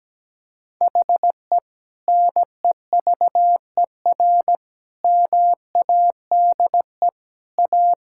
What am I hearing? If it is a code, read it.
Morse code, heard as a keyed tone, about 17 wpm, 700 Hz. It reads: HE NEVER MADE A
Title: Sentences from Top 1600 Words 17wpm